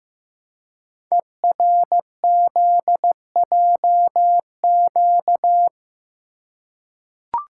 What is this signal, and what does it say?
Morse code, heard as a keyed tone, about 15 wpm, 700 Hz. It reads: ERZJQ
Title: Code Groups - 5 Letters - 15wpm